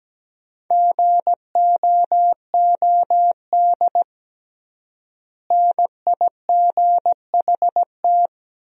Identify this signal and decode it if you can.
Morse code, heard as a keyed tone, about 17 wpm, 700 Hz. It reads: GOOD NIGHT
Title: 3x Word-Spacing - Sentences from Top 300 Words - 17wpm